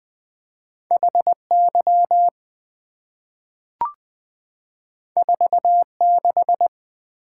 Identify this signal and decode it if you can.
Morse code, heard as a keyed tone, about 20 wpm, 700 Hz. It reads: H Y 4 6
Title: Two Letter-Number Combinations - 20wpm